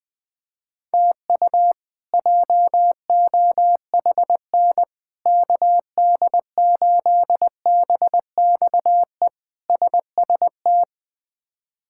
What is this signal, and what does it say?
Morse code, heard as a keyed tone, about 21 wpm, 700 Hz. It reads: TU JOHN KD8BXE SST
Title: K1USN SST QSO Practice - 20wpm